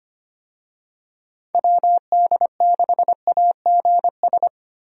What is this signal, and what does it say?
Morse code, heard as a keyed tone, about 25 wpm, 700 Hz. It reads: WD6AGS
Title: USA Call Signs - 6 Characters - Repeated 3x - 25wpm